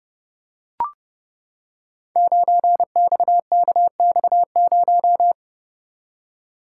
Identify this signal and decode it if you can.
Morse code, heard as a keyed tone, about 30 wpm, 700 Hz. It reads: 9XKX0
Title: Code Groups - 5 Letter:Numbers - 30wpm